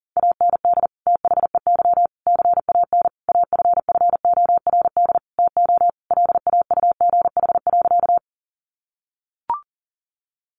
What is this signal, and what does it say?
Morse code, heard as a keyed tone, about 40 wpm, 700 Hz. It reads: AND THEY CAN AFFORD TO LAUGH.
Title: Sentences from Top 1100 Words 40wpm